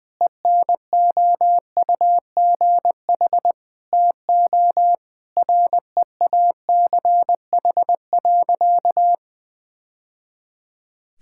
Morse code, heard as a keyed tone, about 21 wpm, 700 Hz. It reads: ENOUGH TO REACH.
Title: Sentences with 2-Word Phrases - Spoken Beginning - 20wpm